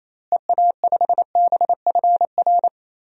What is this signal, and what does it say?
Morse code, heard as a keyed tone, about 28 wpm, 700 Hz. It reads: EA5BFR
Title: DXCC-100 Call Signs - 6 Characters - Repeated 3x - 28wpm